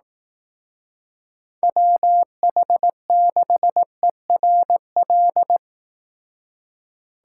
Morse code, heard as a keyed tone, about 18 wpm, 700 Hz. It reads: WH6ERL
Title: Speed Racing - USA 2x3 Call Signs 15wpm